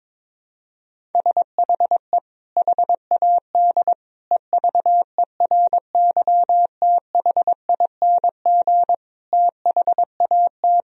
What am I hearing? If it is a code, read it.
Morse code, heard as a keyed tone, about 22 wpm, 700 Hz. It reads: SHE HAD EVERYTHING THAT
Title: Sentences from Top 1000 Words - Review Part 4 - 22wpm